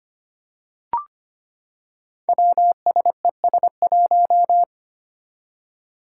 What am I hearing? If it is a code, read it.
Morse code, heard as a keyed tone, about 25 wpm, 700 Hz. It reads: WSES1
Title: Five Challenging Letter-Number Combinations 25wpm